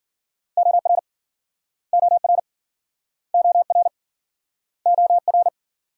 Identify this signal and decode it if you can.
Morse code, heard as a keyed tone, about 57 wpm, 700 Hz. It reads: OR OR OR OR